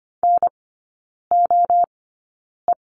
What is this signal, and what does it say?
Morse code, heard as a keyed tone, about 25 wpm, spaced as for 9 wpm, 690 Hz. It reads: NOE
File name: Lesson 14e - Number 2 - Three Characters 25wpm